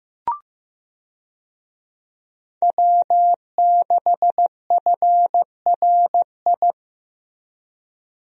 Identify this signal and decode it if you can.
Morse code, heard as a keyed tone, about 16 wpm, 700 Hz. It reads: W6FRI
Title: USA Call Signs - 5 Characters - Repeated 3x - 15wpm